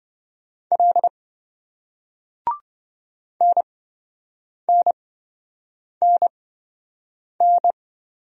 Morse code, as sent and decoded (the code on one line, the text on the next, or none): .-.. / -. -. -. -.
L NNNN